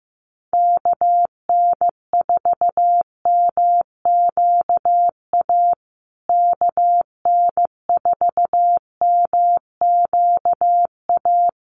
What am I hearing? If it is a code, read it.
Morse code, heard as a keyed tone, about 15 wpm, 700 Hz. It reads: KN4MQA KN4MQA
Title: USA Call Signs Repeated 3x 15wpm